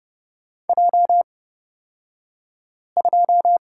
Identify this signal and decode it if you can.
Morse code, heard as a keyed tone, about 30 wpm, spaced as for 5 wpm, 700 Hz. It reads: J2
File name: Lesson 36d - BT - Two Characters 30wpm